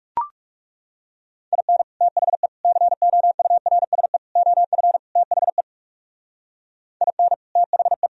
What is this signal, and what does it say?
Morse code, heard as a keyed tone, about 38 wpm, 700 Hz. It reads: IN THE COURSE OF THE IN THE